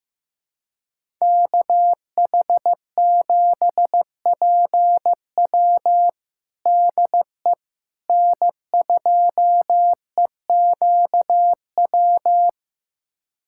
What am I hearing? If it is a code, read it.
Morse code, heard as a keyed tone, about 15 wpm, 700 Hz. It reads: KH7PW DE N2EQW
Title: USA Callsign DE Callsign - 5 Characters - 15wpm